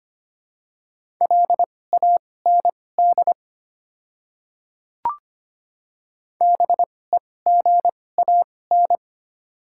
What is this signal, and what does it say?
Morse code, heard as a keyed tone, about 25 wpm, spaced as for 18 wpm, 700 Hz. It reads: LAND BEGAN
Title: 2x Character Spacing - Top 300 Words - 25wpm